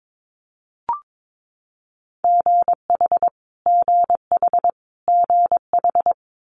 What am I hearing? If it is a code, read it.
Morse code, heard as a keyed tone, about 22 wpm, 700 Hz. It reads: GH GH GH
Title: Multi-letter Phonemes Repeated 3x 22wpm